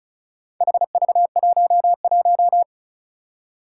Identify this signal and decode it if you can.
Morse code, heard as a keyed tone, about 36 wpm, 700 Hz. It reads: HV11
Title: Four Challenging Letter-Number Combinations 35wpm